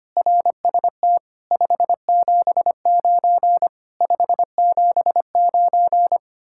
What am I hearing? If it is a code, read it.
Morse code, heard as a keyed tone, about 31 wpm, 700 Hz. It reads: RST 579 579
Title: Speed-Racing Common QSO Phrases - 25wpm